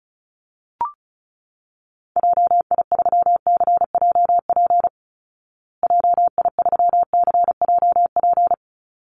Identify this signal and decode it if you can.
Morse code, heard as a keyed tone, about 34 wpm, 700 Hz. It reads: JI3CJP JI3CJP
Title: Call Signs Repeated 3x - 35wpm